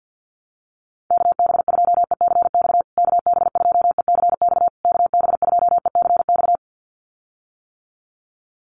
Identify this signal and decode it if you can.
Morse code, heard as a keyed tone, about 50 wpm, 700 Hz. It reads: KB2ECX KB2ECX KB2ECX
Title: USA Call Signs Repeated 3x 50wpm